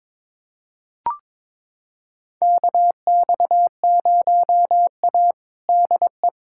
Morse code, heard as a keyed tone, about 21 wpm, 700 Hz. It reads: KX0A DE
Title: USA Callsign DE Callsign 22wpm